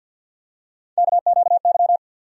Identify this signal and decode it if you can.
Morse code, heard as a keyed tone, about 52 wpm, 700 Hz. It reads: KQY
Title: USA Call Sign Postfixes - 3 Letters 50wpm